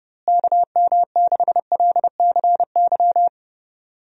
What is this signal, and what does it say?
Morse code, heard as a keyed tone, about 30 wpm, 700 Hz. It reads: KM6LCY
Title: USA Call Signs - 6 Characters - Repeated 2x - 30wpm